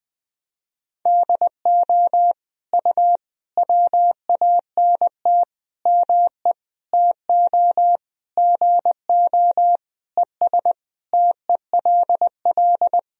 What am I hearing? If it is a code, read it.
Morse code, heard as a keyed tone, about 20 wpm, 700 Hz. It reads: DO U WANT ME TO GO ES TELL
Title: Sentences from Top 200 Words and Abbreviations 20wpm